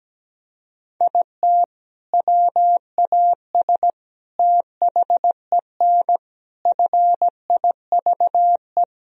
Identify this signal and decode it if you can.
Morse code, heard as a keyed tone, about 17 wpm, 700 Hz. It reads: IT WAS THEN FIVE
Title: Sentences from Top 1000 Words - Review Part 2 - 17wpm